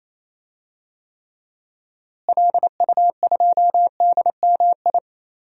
Morse code, heard as a keyed tone, about 27 wpm, 700 Hz. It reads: LU2DMI
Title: DXCC-100 Call Signs - 6 Characters - Repeated 3x - 28wpm